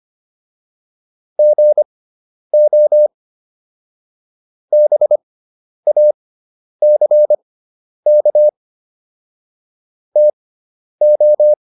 Morse code, heard as a keyed tone, about 25 wpm, spaced as for 10 wpm, 600 Hz. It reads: GO BACK TO